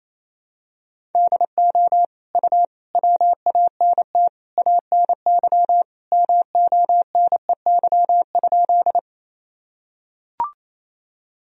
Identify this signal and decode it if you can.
Morse code, heard as a keyed tone, about 28 wpm, 700 Hz. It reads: DO U WANT ANY MONEY?
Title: Sentences from Top 400 Words and Abbreviations 28wpm